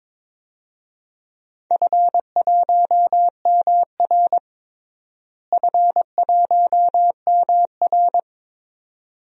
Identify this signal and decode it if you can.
Morse code, heard as a keyed tone, about 22 wpm, 700 Hz. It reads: F1MR F1MR
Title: DXCC-100 Call Signs - 4 Characters - Repeated 2x - 22wpm